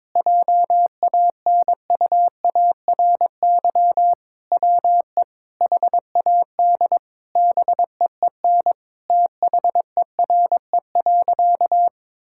Morse code, heard as a keyed tone, about 22 wpm, 700 Hz. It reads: JANUARY WE HAD BEEN THERE.
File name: practice / 22wpm / Sentences with 4-Word Phrases - 22wpm